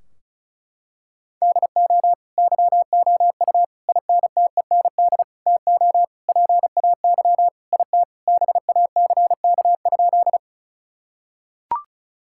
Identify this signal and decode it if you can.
Morse code, heard as a keyed tone, about 34 wpm, 700 Hz. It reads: DO YOU INTEND TO PAY IT BACK?
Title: Sentences from Top 1600 Words 35wpm